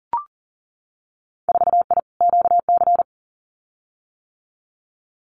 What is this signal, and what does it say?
Morse code, heard as a keyed tone, about 38 wpm, 700 Hz. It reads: 4I QC
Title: Winter Field Day Exchanges Encore 40wpm